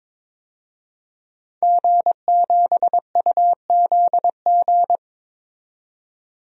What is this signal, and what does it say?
Morse code, heard as a keyed tone, about 22 wpm, 700 Hz. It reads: G7UZG
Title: DXCC-100 Call Signs - 5 Characters - 22wpm